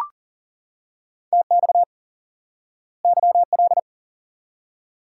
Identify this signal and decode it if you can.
Morse code, heard as a keyed tone, about 40 wpm, 700 Hz. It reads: TX YL
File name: IQE Course 3 - Lesson 89 - YAGI - Two Elements - 40wpm